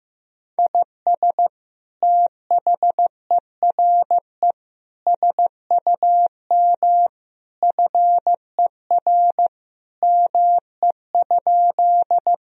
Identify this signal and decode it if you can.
Morse code, heard as a keyed tone, about 15 wpm, 700 Hz. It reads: IS THERE SUM FER ME?
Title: Sentences from Top 200 Words and Abbreviations 15wpm